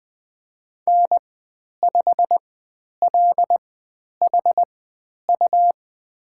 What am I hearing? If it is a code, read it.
Morse code, heard as a keyed tone, about 20 wpm, spaced as for 10 wpm, 700 Hz. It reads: N5LHU
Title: 102 Lesson 17g - Letter u - Call Signs 20wpm